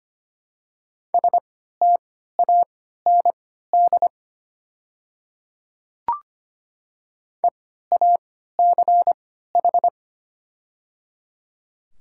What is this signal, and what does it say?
Morse code, heard as a keyed tone, about 25 wpm, spaced as for 14 wpm, 700 Hz. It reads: STAND EACH
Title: 3x Character Spacing - Top 300 Words - 25wpm